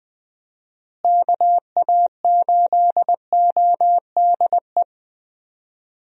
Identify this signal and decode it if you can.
Morse code, heard as a keyed tone, about 21 wpm, 700 Hz. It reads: KA8ODE
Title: USA Call Signs - 6 Characters - Repeated 2x - 20wpm